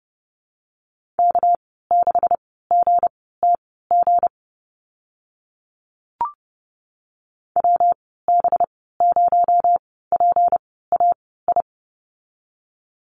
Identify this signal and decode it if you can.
Morse code, heard as a keyed tone, about 30 wpm, spaced as for 17 wpm, 700 Hz. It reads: K6GTG WB0PAI
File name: Lesson 37g - Letter x - Call Signs 30wpm